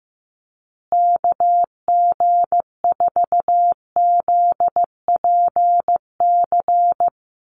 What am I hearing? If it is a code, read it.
Morse code, heard as a keyed tone, about 15 wpm, 700 Hz. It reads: KG4ZPC
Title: USA Call Signs Repeated 2x 15wpm